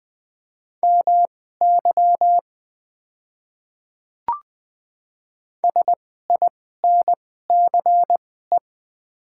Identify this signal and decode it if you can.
Morse code, heard as a keyed tone, about 20 wpm, spaced as for 14 wpm, 700 Hz. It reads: MY SINCE
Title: 2x Character Spacing - Top 300 Words - 20wpm